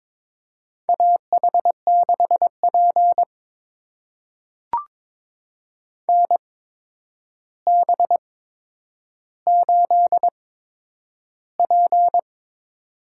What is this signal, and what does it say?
Morse code, heard as a keyed tone, about 22 wpm, spaced as for 9 wpm, 700 Hz. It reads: AH6P NB8P
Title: USA Call Signs - 4 Characters - 8x Character Spacing - 22wpm